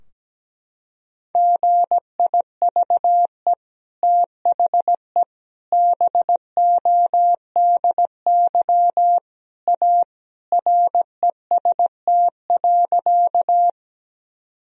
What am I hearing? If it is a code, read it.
Morse code, heard as a keyed tone, about 17 wpm, 700 Hz. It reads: GIVE THE BODY A REST.
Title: Sentences from Top 1000 Words - Review Part 2 - 17wpm